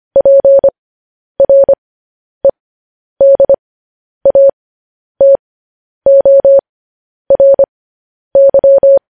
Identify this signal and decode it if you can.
Morse code, heard as a keyed tone, about 25 wpm, spaced as for 10 wpm, 550 Hz. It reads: PREDATORY
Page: news headlines in morse code at 10 wpm